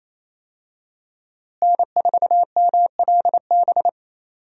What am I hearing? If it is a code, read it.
Morse code, heard as a keyed tone, about 28 wpm, 700 Hz. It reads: N4MLB
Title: USA Call Signs - 5 Characters - Repeated 3x - 28wpm